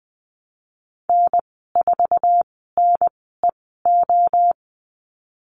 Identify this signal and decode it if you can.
Morse code, heard as a keyed tone, about 20 wpm, spaced as for 15 wpm, 700 Hz. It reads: N4NEO
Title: 147 Lesson 24g - Letter y - Call Signs 20wpm